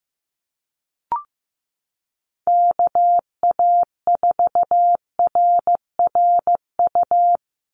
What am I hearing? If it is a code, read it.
Morse code, heard as a keyed tone, about 15 wpm, 700 Hz. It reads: KA4RRU
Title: Call Signs Repeated 3x - 15wpm